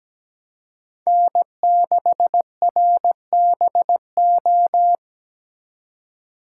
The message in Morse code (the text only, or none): N6RBO